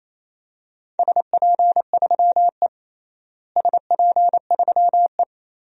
Morse code, heard as a keyed tone, about 28 wpm, 700 Hz. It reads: SP3E SP3E
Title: DXCC-100 Call Signs - 4 Characters - Repeated 3x - 28wpm